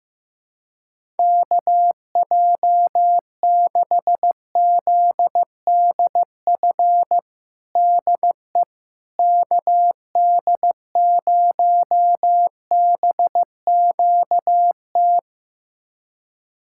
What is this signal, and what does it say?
Morse code, heard as a keyed tone, about 15 wpm, 700 Hz. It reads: KJ6ZDF DE KD0BQT
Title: USA Callsign DE Callsign - 6 Characters - 15wpm